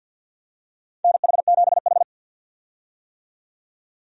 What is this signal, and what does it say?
Morse code, heard as a keyed tone, about 52 wpm, 700 Hz. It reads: NH7H
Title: USA Call Signs - 4 Characters - 50wpm